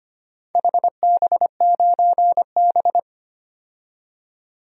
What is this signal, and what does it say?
Morse code, heard as a keyed tone, about 25 wpm, 700 Hz. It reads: HB9B
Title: DXCC-100 Call Signs - 4 Characters - 25wpm